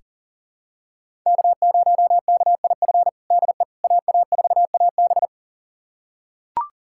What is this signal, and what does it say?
Morse code, heard as a keyed tone, about 40 wpm, 690 Hz. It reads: K0KIF DE AA4AB
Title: USA Callsign DE Callsign - 5 Characters - 40wpm